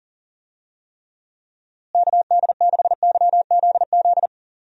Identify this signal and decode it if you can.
Morse code, heard as a keyed tone, about 40 wpm, 700 Hz. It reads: KD6YZZ
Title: USA Call Signs - 6 Characters - Repeated 3x - 40wpm